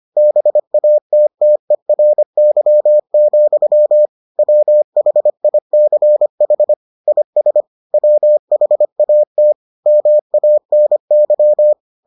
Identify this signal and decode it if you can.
Morse code, heard as a keyed tone, about 25 wpm, 600 Hz. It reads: BATTERY, WHICH IS WHAT MANY